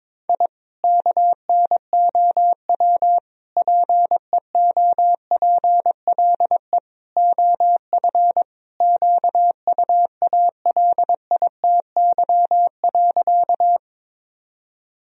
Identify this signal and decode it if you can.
Morse code, heard as a keyed tone, about 22 wpm, 700 Hz. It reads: I KNOW PEOPLE OF QUALITY.
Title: Sentences from Top 1000 Words - Review Part 4 - 22wpm